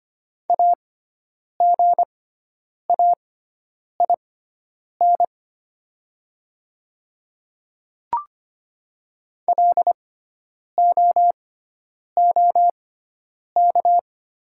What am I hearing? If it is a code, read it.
Morse code, heard as a keyed tone, about 25 wpm, spaced as for 9 wpm, 700 Hz. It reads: AGAIN LOOK
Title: 6x Character Spacing - Top 200 Words - 25wpm